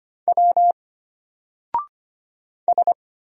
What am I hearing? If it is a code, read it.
Morse code, heard as a keyed tone, about 25 wpm, 700 Hz. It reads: W S
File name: Single-Letter-Number-25wpm